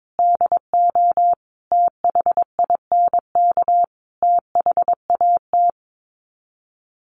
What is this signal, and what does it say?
Morse code, heard as a keyed tone, about 22 wpm, 700 Hz. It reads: DO THINK THAT
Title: Sets of 5 Words Encore2 22wpm